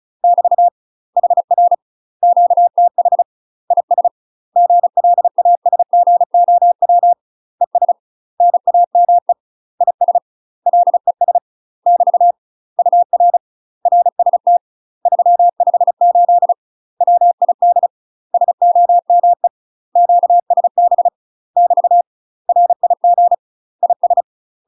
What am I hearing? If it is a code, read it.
Morse code, heard as a keyed tone, about 36 wpm, 700 Hz. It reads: <BT> HR QTH IS GLASGOW ES NAME IS LES <BT> UR RST 358 WID SOME QSB <BT> RIG IS
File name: UK MASTER QSOs TRAINING / 35WPM / UK MASTER QSOs 0010